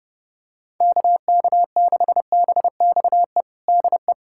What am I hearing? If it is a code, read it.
Morse code, heard as a keyed tone, about 30 wpm, 700 Hz. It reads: KK6BXE DE
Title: USA Callsign DE Callsign - 6 Characters - 30wpm